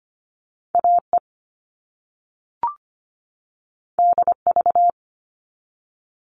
Morse code, heard as a keyed tone, about 25 wpm, 700 Hz. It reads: AE DV
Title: All Cut Numbers - 2-Digit 25wpm